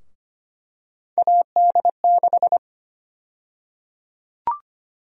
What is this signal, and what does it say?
Morse code, heard as a keyed tone, about 25 wpm, 700 Hz. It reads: AD6
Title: USA Call Sign Prefixes - Repeated 2x - 25wpm